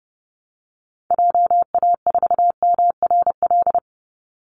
Morse code, heard as a keyed tone, about 30 wpm, 700 Hz. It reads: JA4MRL
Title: Call Signs Repeated 3x - 30wpm